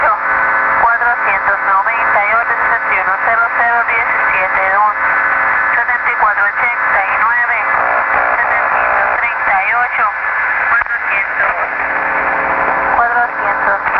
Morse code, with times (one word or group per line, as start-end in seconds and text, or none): none